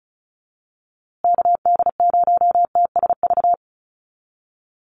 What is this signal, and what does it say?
Morse code, heard as a keyed tone, about 34 wpm, 700 Hz. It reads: KD0TSV
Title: USA Call Signs Repeated 2x 35wpm